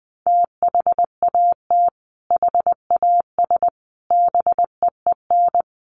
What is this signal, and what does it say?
Morse code, heard as a keyed tone, about 20 wpm, 700 Hz. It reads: THAT HAS BEEN
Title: Sentences from Top 700 Words 20wpm